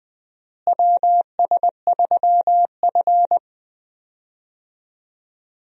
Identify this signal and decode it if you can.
Morse code, heard as a keyed tone, about 20 wpm, 700 Hz. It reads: WS3F
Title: USA Call Signs - 4 Characters - Repeated 2x - 20wpm